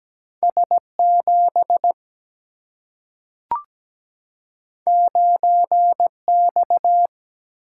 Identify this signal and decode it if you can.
Morse code, heard as a keyed tone, about 17 wpm, 700 Hz. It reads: S 7 9 X
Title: Single-Letter Single-Number 17wpm